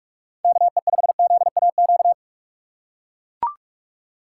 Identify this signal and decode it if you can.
Morse code, heard as a keyed tone, about 44 wpm, 700 Hz. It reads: KE5ZAQ
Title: USA Call Signs - 6 Characters - 6x Character Spacing - 45wpm